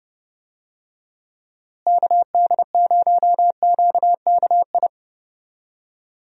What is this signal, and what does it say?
Morse code, heard as a keyed tone, about 30 wpm, 700 Hz. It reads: KD0QKI
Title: USA Call Signs - 6 Characters - Repeated 2x - 30wpm